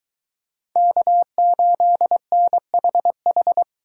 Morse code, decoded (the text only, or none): K8NHH